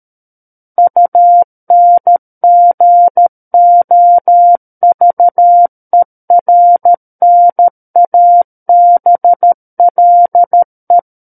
Morse code, heard as a keyed tone, about 13 wpm, 700 Hz. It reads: UNGOVERNABLE
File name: Book One / 04-Chapter 4 - 13 WPM